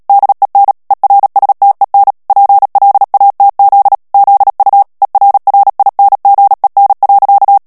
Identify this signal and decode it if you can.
Morse code, heard as a keyed tone, about 37 wpm, 800 Hz. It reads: DEN ERSTEN PLATZ ZU ERRINGEN.
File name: morse mp3 / ALLEMAND texte clair 2 vitesse 35 mots minute